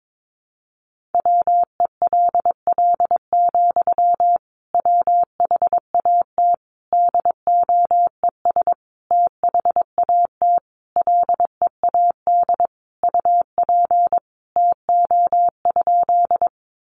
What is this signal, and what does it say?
Morse code, heard as a keyed tone, about 22 wpm, 700 Hz. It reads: WELL, WHAT DOES THAT LEAD UP TO?
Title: Sentences from Top 500 Words 22wpm